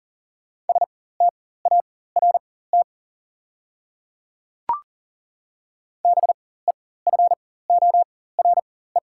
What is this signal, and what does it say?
Morse code, heard as a keyed tone, about 40 wpm, spaced as for 19 wpm, 700 Hz. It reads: START BEFORE
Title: 4x Character Spacing - Top 300 Words - 40wpm